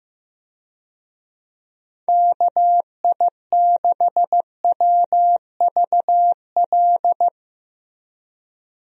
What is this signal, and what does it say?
Morse code, heard as a keyed tone, about 15 wpm, 700 Hz. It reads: KI6WVL
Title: USA Call Signs - 6 Characters - Repeated 3x - 15wpm